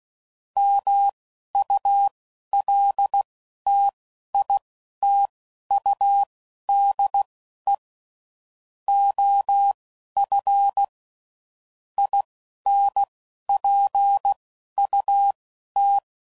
Morse code, heard as a keyed tone, about 16 wpm, spaced as for 11 wpm, 800 Hz. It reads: MULTITUDE OF INPUT
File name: CW Code Groups @10WPM / Technical Subjects / OR Gate